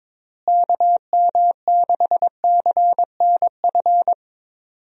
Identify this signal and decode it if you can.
Morse code, heard as a keyed tone, about 22 wpm, 700 Hz. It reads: KM6CNF
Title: USA Call Signs - 6 Characters - Repeated 2x - 22wpm